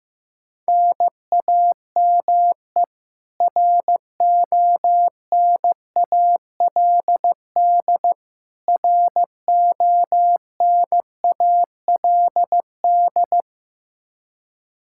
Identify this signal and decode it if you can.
Morse code, heard as a keyed tone, about 15 wpm, 700 Hz. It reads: NAME RONALD RONALD
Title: Practice QSO - Exchange 1 - Beginning - 15wpm